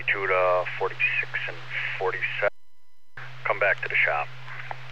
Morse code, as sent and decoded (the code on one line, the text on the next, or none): none